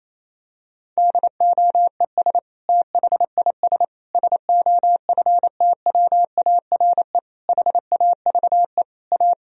Morse code, heard as a keyed tone, about 27 wpm, 700 Hz. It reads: DOES THIS SOFTWARE HAVE A
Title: Sentences from Top 1000 Words - Review Part 3 - 28wpm